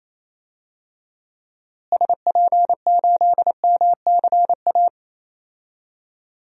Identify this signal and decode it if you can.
Morse code, heard as a keyed tone, about 28 wpm, 700 Hz. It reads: SP8MCA
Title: DXCC-100 Call Signs - 6 Characters - Repeated 3x - 28wpm